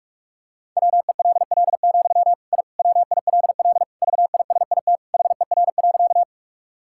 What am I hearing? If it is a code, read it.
Morse code, heard as a keyed tone, about 46 wpm, 690 Hz. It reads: WELL, I WILL VISIT HER.
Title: Sentences from Top 1000 Words - Review Part 6 - 45wpm